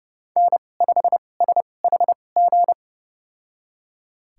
Morse code, heard as a keyed tone, about 32 wpm, spaced as for 22 wpm, 700 Hz. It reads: N5SHG
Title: USA Call Signs - 5 Characters - 2x Character Spacing - 30wpm